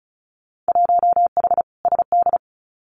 Morse code, heard as a keyed tone, about 36 wpm, 690 Hz. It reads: 1H SD
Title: Winter Field Day Exchanges 35wpm